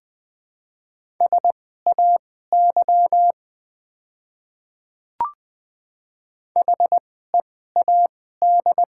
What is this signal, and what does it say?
Morse code, heard as a keyed tone, about 20 wpm, spaced as for 14 wpm, 700 Hz. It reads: SAY HEAD
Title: 2x Character Spacing - Top 300 Words - 20wpm